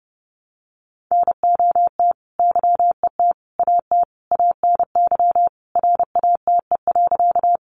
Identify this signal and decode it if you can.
Morse code, heard as a keyed tone, about 30 wpm, 700 Hz. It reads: NOT YET AT ANY RATE.
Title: Sentences from Top 700 Words 30wpm